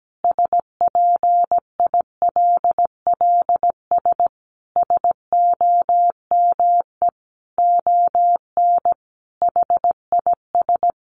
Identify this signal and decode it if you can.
Morse code, heard as a keyed tone, about 17 wpm, 700 Hz. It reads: SPILLS SOME ON HIS
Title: Additional Easy Phrases 17wpm